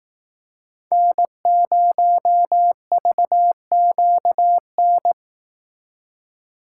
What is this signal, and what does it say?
Morse code, heard as a keyed tone, about 18 wpm, 700 Hz. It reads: N0VQN
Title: Speed Racing - USA 1x3 Call Signs 15wpm